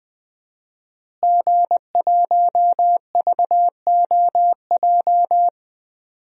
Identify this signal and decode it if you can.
Morse code, heard as a keyed tone, about 20 wpm, 700 Hz. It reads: G1VOJ
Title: DXCC-100 Call Signs - 5 Characters - Repeated 3x - 20wpm